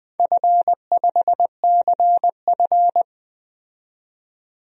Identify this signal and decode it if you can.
Morse code, heard as a keyed tone, about 20 wpm, 700 Hz. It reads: F5CF
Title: DXCC-100 Call Signs - 4 Characters - 20wpm